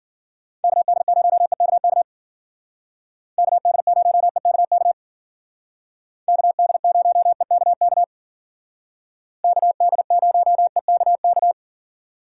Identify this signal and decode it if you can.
Morse code, heard as a keyed tone, about 52 wpm, 700 Hz. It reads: KD0EKK KD0EKK KD0EKK KD0EKK